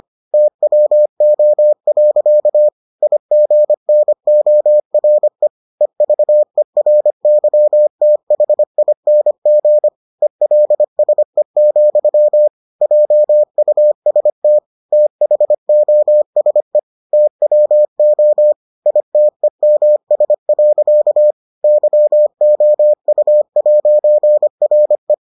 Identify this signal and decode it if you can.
Morse code, heard as a keyed tone, about 25 wpm, 600 Hz. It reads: TWO. IGNORE EVERYTHING ELSE, JUST THOSE TWO ITEMS. YOU'RE